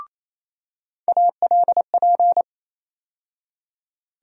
Morse code, heard as a keyed tone, about 27 wpm, 700 Hz. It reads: ALP
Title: USA Call Sign Postfixes - 3 Letters 28wpm